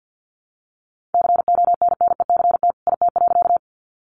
Morse code, heard as a keyed tone, about 50 wpm, 700 Hz. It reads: CONNECT IT.
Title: Sentences from Top 1500 Words 50wpm